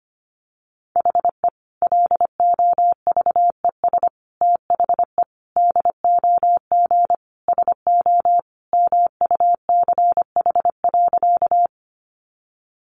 Morse code, heard as a keyed tone, about 25 wpm, 700 Hz. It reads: HE LOVES THE DOG SO MUCH.
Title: Additional Easy Phrases 25wpm